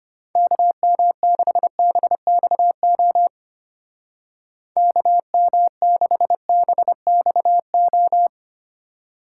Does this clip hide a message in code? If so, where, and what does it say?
Yes: Morse code from 0.4 to 8.3 s, heard as a keyed tone, about 32 wpm, 700 Hz. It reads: KM6BXO KM6BXO